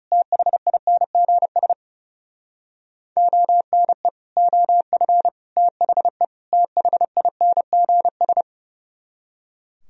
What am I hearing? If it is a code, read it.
Morse code, heard as a keyed tone, about 35 wpm, 700 Hz. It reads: THINGS ONE OF THE THINGS
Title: Speed-Racing - Sets of 4 Words 30wpm